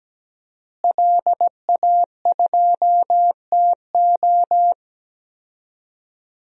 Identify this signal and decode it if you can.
Morse code, heard as a keyed tone, about 17 wpm, 700 Hz. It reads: LA2TO
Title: DXCC-100 Call Signs - 5 Characters - Repeated 2x - 17wpm